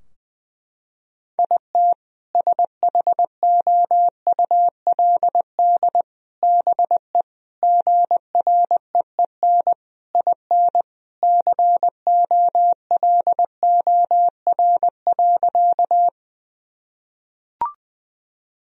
Morse code, heard as a keyed tone, about 20 wpm, 700 Hz. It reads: IT SHOULD BE GREEN IN COLOR.
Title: Sentences from Top 1000 Words - Review Part 2 - 20wpm